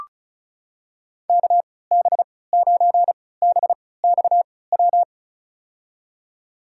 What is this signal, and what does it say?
Morse code, heard as a keyed tone, about 36 wpm, spaced as for 20 wpm, 690 Hz. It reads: KB9BXW
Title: USA Call Signs - 3x Character Spacing - 35wpm